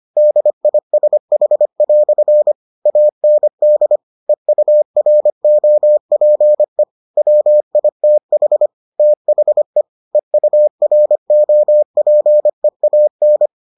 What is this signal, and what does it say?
Morse code, heard as a keyed tone, about 25 wpm, 600 Hz. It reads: DISH" AND EUROPE WITH THE EUROPEAN